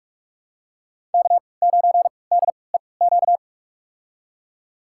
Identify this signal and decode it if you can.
Morse code, heard as a keyed tone, about 46 wpm, spaced as for 26 wpm, 700 Hz. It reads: K9DEQ
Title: USA Call Signs - 5 Characters - 3x Character Spacing - 45wpm